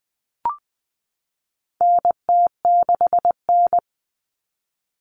Morse code, heard as a keyed tone, about 20 wpm, 700 Hz. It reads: NT6N
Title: All Cut Numbers - 4-Digit 20wpm